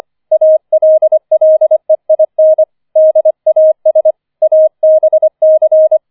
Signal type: Morse code